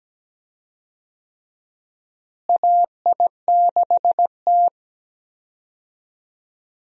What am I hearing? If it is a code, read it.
Morse code, heard as a keyed tone, about 17 wpm, 700 Hz. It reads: AI6T
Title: USA Call Signs - 4 Characters - Repeated 3x - 17wpm